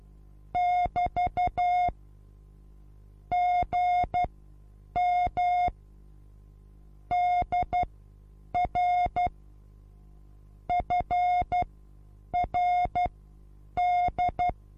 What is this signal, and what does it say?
Morse code, heard as a keyed tone, about 12 wpm, spaced as for 8 wpm, 720 Hz. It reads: <BT> GM DR FRD
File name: repos / Morsekurs / DJ4UF Morsen23